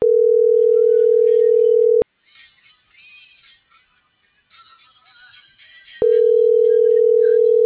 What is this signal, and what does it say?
Ringback tone (North America), 440 + 480 Hz.